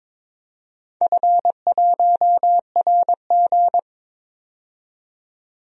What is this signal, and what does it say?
Morse code, heard as a keyed tone, about 21 wpm, 700 Hz. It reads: F1RG
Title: DXCC-100 Call Signs - 4 Characters - 22wpm